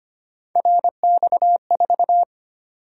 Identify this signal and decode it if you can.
Morse code, heard as a keyed tone, about 25 wpm, 700 Hz. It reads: RX4
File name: Three Letter-Number Combinations - 25wpm